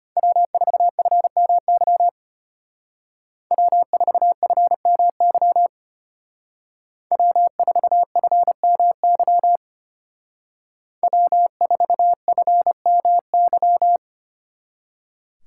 Morse code, heard as a keyed tone, about 33 wpm, 700 Hz. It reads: W4FMY W4FMY W4FMY W4FMY